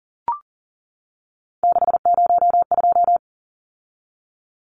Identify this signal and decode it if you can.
Morse code, heard as a keyed tone, about 40 wpm, 700 Hz. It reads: 602
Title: Numbers - 3-Digit 40wpm